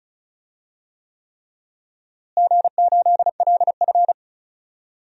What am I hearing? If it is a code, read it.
Morse code, heard as a keyed tone, about 36 wpm, 700 Hz. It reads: G8LF
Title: DXCC-100 Call Signs - 4 Characters - Repeated 3x - 35wpm